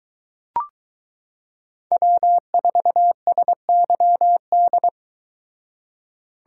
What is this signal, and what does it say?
Morse code, heard as a keyed tone, about 23 wpm, 700 Hz. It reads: W4SYD